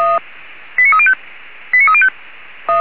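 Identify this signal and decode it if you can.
Five-tone selective call (ZVEI), heard as quick runs of five tones, tones of about 70 ms. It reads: 89305 89305
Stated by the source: ZVEI five-tone call received on an SDR.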